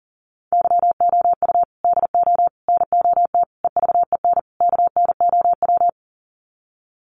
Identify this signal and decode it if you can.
Morse code, heard as a keyed tone, about 37 wpm, 700 Hz. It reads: YOU DO NOT EVEN KNOW
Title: Sets of 5 Words Encore2 40wpm